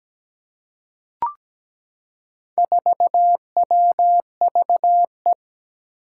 Morse code, heard as a keyed tone, about 17 wpm, 700 Hz. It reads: 4WVE
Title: Four Challenging Letter-Number Combinations 17wpm